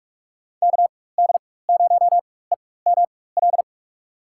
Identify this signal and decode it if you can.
Morse code, heard as a keyed tone, about 44 wpm, spaced as for 21 wpm, 690 Hz. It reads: KD0EML
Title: USA Call Signs - 4x Character Spacing - 45wpm